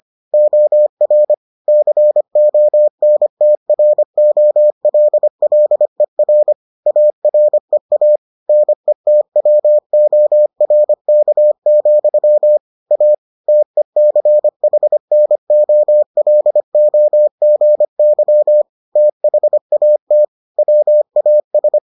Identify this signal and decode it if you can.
Morse code, heard as a keyed tone, about 25 wpm, 600 Hz. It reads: OR CONTROLLER AREA NETWORK, A TECHNOLOGY THAT WAS